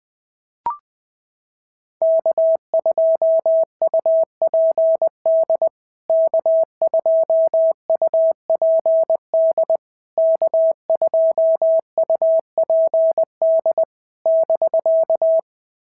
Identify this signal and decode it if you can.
Morse code, heard as a keyed tone, about 20 wpm, 650 Hz. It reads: K2UPD K2UPD K2UPD <BK>
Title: POTA QSO Practice - 20wpm